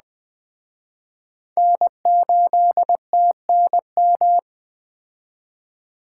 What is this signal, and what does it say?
Morse code, heard as a keyed tone, about 20 wpm, 700 Hz. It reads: N8TNM
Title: Speed Racing - USA 1x3 Call Signs 17wpm